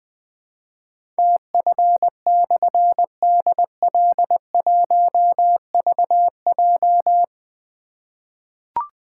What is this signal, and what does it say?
Morse code, heard as a keyed tone, about 20 wpm, 700 Hz. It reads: TF/DL1VJ
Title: Difficult Call Signs 20wpm